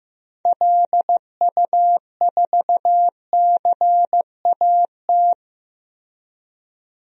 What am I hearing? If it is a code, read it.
Morse code, heard as a keyed tone, about 15 wpm, 700 Hz. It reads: LU4CAT
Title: DXCC-100 Call Signs - 6 Characters - Repeated 2x - 15wpm